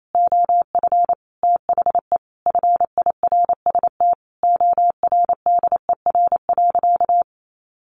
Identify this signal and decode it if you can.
Morse code, heard as a keyed tone, about 28 wpm, 700 Hz. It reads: OF THE FIRST ORDER.
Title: Sentences from Top 1100 Words 28wpm